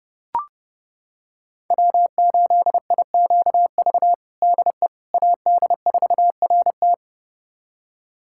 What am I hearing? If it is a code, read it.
Morse code, heard as a keyed tone, about 32 wpm, 700 Hz. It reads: W8IQV DE AD4RT
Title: USA Callsign DE Callsign 30wpm